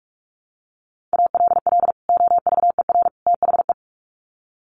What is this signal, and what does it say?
Morse code, heard as a keyed tone, about 44 wpm, 700 Hz. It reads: ALL OVER THE
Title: Sets of 3 Words 45wpm